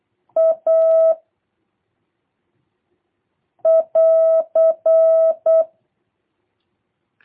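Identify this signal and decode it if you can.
Morse code, heard as a keyed tone, about 8 wpm, 650 Hz. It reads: A <AR>
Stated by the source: Morse copy drill by JH8CBH: 24 random characters, A, E, N, T six times each, 3-second gaps for seven questions, 2-second gaps for three.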